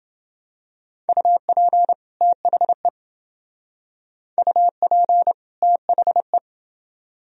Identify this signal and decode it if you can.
Morse code, heard as a keyed tone, about 28 wpm, 700 Hz. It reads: UP THE UP THE